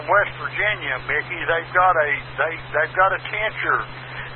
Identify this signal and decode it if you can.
Emergency Alert System (SAME) tones, heard as harsh buzzing bursts, with speech. It is heard from 1.0 to 3.7 s.